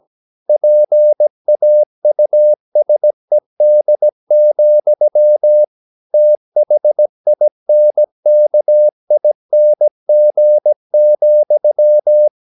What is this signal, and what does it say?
Morse code, heard as a keyed tone, about 17 wpm, 600 Hz. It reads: PAUSED, THINKING,
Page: Romanen Never af Ken Follet på engelsk.